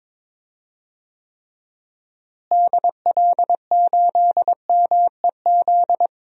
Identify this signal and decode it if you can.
Morse code, heard as a keyed tone, about 22 wpm, 700 Hz. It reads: DL8MEZ
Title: DXCC-100 Call Signs - 6 Characters - Repeated 2x - 22wpm